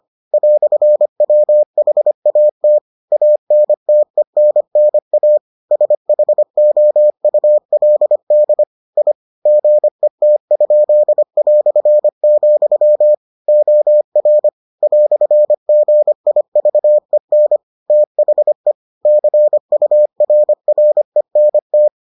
Morse code, heard as a keyed tone, about 25 wpm, 600 Hz. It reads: "WHAT ANTENNA SHOULD I GET?", OR "GIVEN THE CURRENT